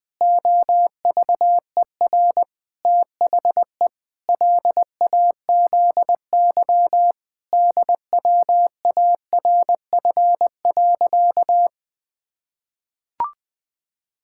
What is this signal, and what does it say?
Morse code, heard as a keyed tone, about 20 wpm, 700 Hz. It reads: OVER THE LAZY DWARF.
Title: Pangram Challenge v2 - 20wpm